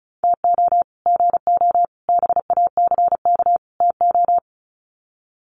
Morse code, heard as a keyed tone, about 34 wpm, 700 Hz. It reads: TO GO BACK TO
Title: Sets of 4 Words 35wpm